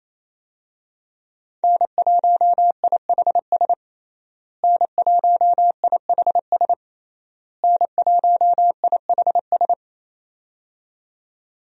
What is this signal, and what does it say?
Morse code, heard as a keyed tone, about 27 wpm, 700 Hz. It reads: N1IHS N1IHS N1IHS
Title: USA Call Signs - 5 Characters - Repeated 3x - 28wpm